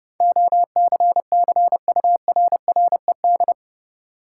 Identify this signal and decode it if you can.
Morse code, heard as a keyed tone, about 32 wpm, 700 Hz. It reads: OCCURRED
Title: Sets of 5 Words Encore - New Word Drill - 30wpm